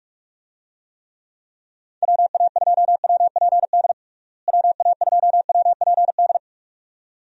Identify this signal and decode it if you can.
Morse code, heard as a keyed tone, about 46 wpm, 700 Hz. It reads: WA2WPD WA2WPD
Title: USA Call Signs - 6 Characters - Repeated 2x - 45wpm